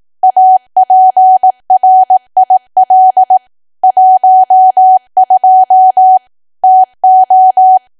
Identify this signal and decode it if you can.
Morse code, heard as a keyed tone, about 18 wpm, 750 Hz. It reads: APRIL 12 TO
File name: cwa / W1AW-Level2 / 18WPM